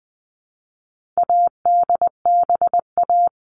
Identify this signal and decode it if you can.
Morse code, heard as a keyed tone, about 20 wpm, 700 Hz. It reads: ADBA
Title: All Cut Numbers - 4-Digit 20wpm